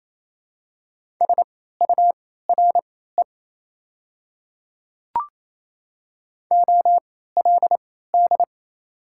Morse code, heard as a keyed tone, about 28 wpm, spaced as for 16 wpm, 700 Hz. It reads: SURE OLD
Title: 3x Character Spacing - Top 300 Words - 28wpm